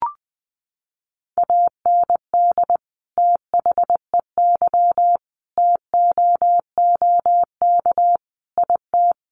Morse code, heard as a keyed tone, about 20 wpm, 700 Hz. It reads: AND THEY TOOK IT
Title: Sentences from Top 1400 Words 20wpm